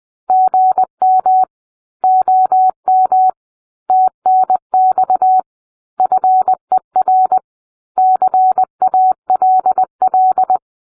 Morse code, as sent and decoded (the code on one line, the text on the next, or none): --. -- / --- -- / - -. -..- / ..-. .-. / -.-. .- .-.. .-..
GM OM TNX FER CALL